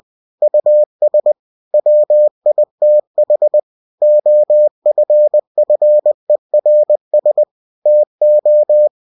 Morse code, heard as a keyed tone, about 20 wpm, 600 Hz. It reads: US WITH OFFERS TO